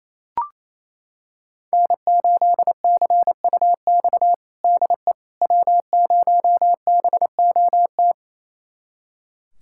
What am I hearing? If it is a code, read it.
Morse code, heard as a keyed tone, about 28 wpm, 700 Hz. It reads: N8CUX DE W0BOT
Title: USA Callsign DE Callsign - 5 Characters - 28wpm